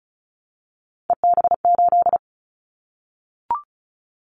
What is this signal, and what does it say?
Morse code, heard as a keyed tone, about 36 wpm, 700 Hz. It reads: EB8
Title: Callsign Prefixes for Top 50 DXCC - 35wpm